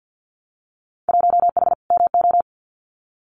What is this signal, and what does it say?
Morse code, heard as a keyed tone, about 50 wpm, 700 Hz. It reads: 1H MO
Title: Winter Field Day Exchanges 50wpm